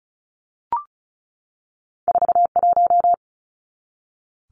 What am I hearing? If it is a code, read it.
Morse code, heard as a keyed tone, about 36 wpm, 700 Hz. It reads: 41
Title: Numbers - 2-Digit 35wpm